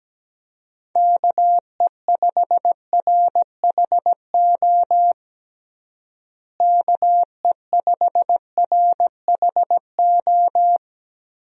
Morse code, heard as a keyed tone, about 17 wpm, 700 Hz. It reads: KE5RHO KE5RHO
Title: USA Call Signs - 6 Characters - Repeated 3x - 17wpm